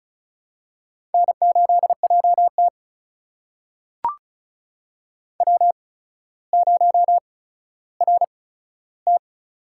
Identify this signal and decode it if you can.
Morse code, heard as a keyed tone, about 36 wpm, spaced as for 15 wpm, 700 Hz. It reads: N8JT W0RT
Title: USA Call Signs - 4 Characters - 8x Character Spacing - 35wpm